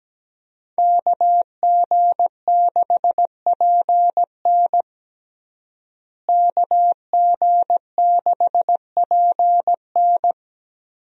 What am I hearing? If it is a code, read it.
Morse code, heard as a keyed tone, about 17 wpm, 700 Hz. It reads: KG6PN KG6PN
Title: USA Call Signs - 5 Characters - Repeated 3x - 17wpm